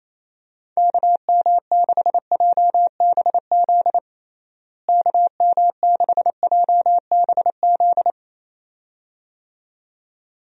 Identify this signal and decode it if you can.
Morse code, heard as a keyed tone, about 27 wpm, 700 Hz. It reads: KM6JBZ KM6JBZ
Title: USA Call Signs - 6 Characters - Repeated 2x - 28wpm